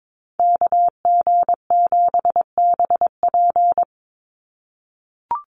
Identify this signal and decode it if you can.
Morse code, heard as a keyed tone, about 22 wpm, 700 Hz. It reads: KG7BP
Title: USA Call Signs 22wpm